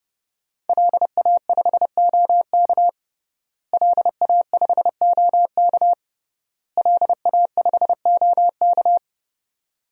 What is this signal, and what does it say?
Morse code, heard as a keyed tone, about 32 wpm, 700 Hz. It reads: LA5OK LA5OK LA5OK
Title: DXCC-100 Call Signs - 5 Characters - Repeated 3x - 30wpm